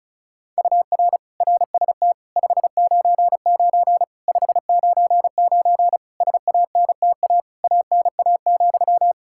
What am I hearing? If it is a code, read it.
Morse code, heard as a keyed tone, about 36 wpm, 700 Hz. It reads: UR RST 599 599 SANTA ANA,
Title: Practice QSO - Exchange 1 - Beginning - 35wpm